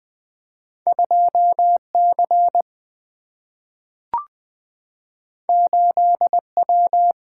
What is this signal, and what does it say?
Morse code, heard as a keyed tone, about 20 wpm, 700 Hz. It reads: 2C 8W
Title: Single-Letter Single-Number 20wpm